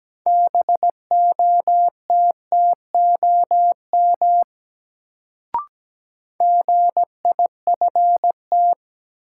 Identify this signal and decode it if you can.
Morse code, heard as a keyed tone, about 17 wpm, 700 Hz. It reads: BOTTOM GIFT
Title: Sentences from Top 900 Words - New Word Drill - 17wpm